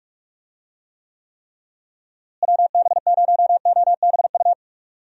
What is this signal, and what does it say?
Morse code, heard as a keyed tone, about 41 wpm, 700 Hz. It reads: WB0OBU
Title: USA Call Signs - 6 Characters - Repeated 3x - 45wpm